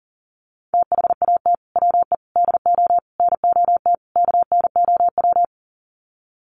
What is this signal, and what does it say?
Morse code, heard as a keyed tone, about 40 wpm, 700 Hz. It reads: THAT WE DO NOT KNOW
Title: Sets of 5 Words Encore2 40wpm